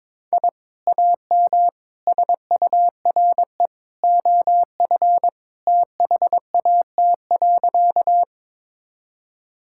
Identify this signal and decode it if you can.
Morse code, heard as a keyed tone, about 21 wpm, 700 Hz. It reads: I AM SURE OF THAT.
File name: practice / 22wpm / Sentences with 2-Word Phrases - 22wpm